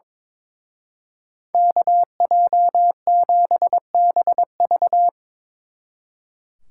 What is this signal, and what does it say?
Morse code, heard as a keyed tone, about 25 wpm, 700 Hz. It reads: KJ7BV
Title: Speed Racing - USA 2x2 Call Signs 22wpm